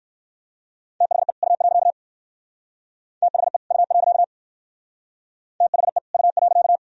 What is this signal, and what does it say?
Morse code, heard as a keyed tone, about 55 wpm, 700 Hz. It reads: THE U. THE U. THE U.
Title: Speed-Racing - Sets of 2 Words 45wpm